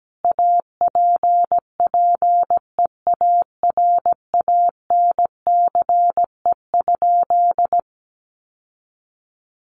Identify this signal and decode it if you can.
Morse code, heard as a keyed tone, about 17 wpm, 700 Hz. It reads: APPEARANCE?
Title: Sentences from Top 1100 Words 17wpm